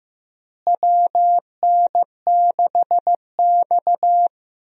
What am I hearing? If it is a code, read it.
Morse code, heard as a keyed tone, about 15 wpm, 700 Hz. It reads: WN6X
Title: USA Call Signs - 4 Characters - Repeated 3x - 15wpm